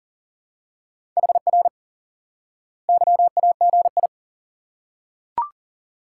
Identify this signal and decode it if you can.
Morse code, heard as a keyed tone, about 40 wpm, 700 Hz. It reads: HR YAGI
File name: IQE Course 3 - Lesson 89 - YAGI - Two Elements - 40wpm